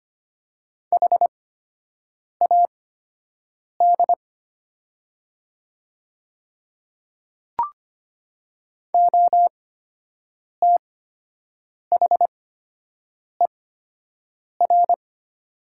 Morse code, heard as a keyed tone, about 25 wpm, spaced as for 7 wpm, 700 Hz. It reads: HAD OTHER
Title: 8x Character Spacing - Top 200 Words - 25wpm